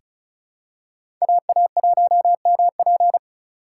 Morse code, heard as a keyed tone, about 34 wpm, 700 Hz. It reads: AA1MP
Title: USA Call Signs - 5 Characters - 3x Character Spacing - 35wpm